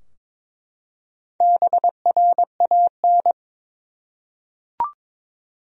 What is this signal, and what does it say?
Morse code, heard as a keyed tone, about 22 wpm, 700 Hz. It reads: BRAN
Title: USA Common Names - 22wpm